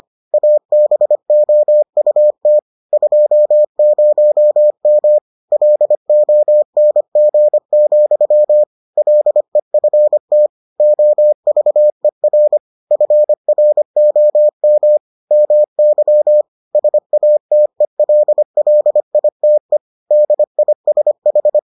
Morse code, heard as a keyed tone, about 25 wpm, 600 Hz. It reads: ABOUT 20M LONG, LEFT OVER FROM MY SATELLITE DISH